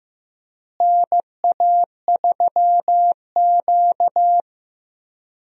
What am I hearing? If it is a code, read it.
Morse code, heard as a keyed tone, about 15 wpm, 700 Hz. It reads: NA3Q
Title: USA Call Signs - 4 Characters - Repeated 3x - 15wpm